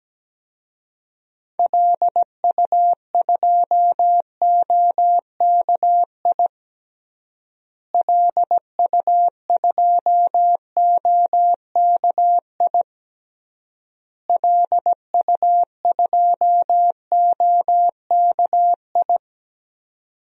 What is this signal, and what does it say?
Morse code, heard as a keyed tone, about 17 wpm, 700 Hz. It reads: LU2OKI LU2OKI LU2OKI
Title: DXCC-100 Call Signs - 6 Characters - Repeated 3x - 17wpm